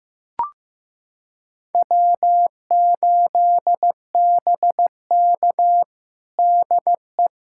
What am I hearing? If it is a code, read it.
Morse code, heard as a keyed tone, about 15 wpm, 700 Hz. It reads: W8BK DE
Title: USA Callsign DE Callsign - 4 Characters - 15wpm